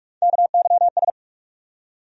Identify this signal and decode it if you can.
Morse code, heard as a keyed tone, about 46 wpm, 700 Hz. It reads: KYS
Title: USA Call Sign Postfixes - 3 Letters 45wpm